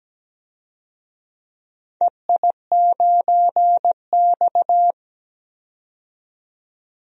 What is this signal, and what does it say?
Morse code, heard as a keyed tone, about 17 wpm, 700 Hz. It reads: EI9X
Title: DXCC-100 Call Signs - 4 Characters - 17wpm